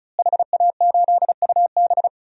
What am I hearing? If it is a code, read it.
Morse code, heard as a keyed tone, about 34 wpm, 700 Hz. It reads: HA8UB
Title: DXCC-100 Call Signs - 5 Characters - Repeated 3x - 35wpm